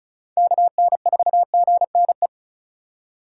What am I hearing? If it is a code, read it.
Morse code, heard as a keyed tone, about 34 wpm, 700 Hz. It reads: KN4GNE
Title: USA Call Signs - 2x Character Spacing - 35wpm